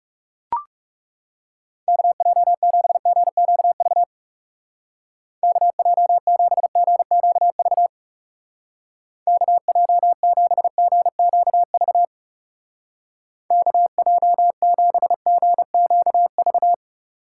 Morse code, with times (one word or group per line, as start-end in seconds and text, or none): 1.9-4.0 KJ7GQV
5.4-7.9 KJ7GQV
9.3-12.0 KJ7GQV
13.5-16.7 KJ7GQV